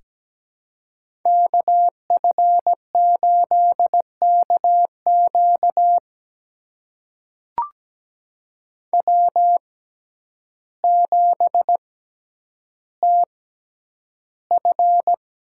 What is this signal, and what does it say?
Morse code, heard as a keyed tone, about 17 wpm, spaced as for 9 wpm, 700 Hz. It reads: KF8KQ W7TF
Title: USA Call Signs - 5 Characters - 6x Character Spacing - 17wpm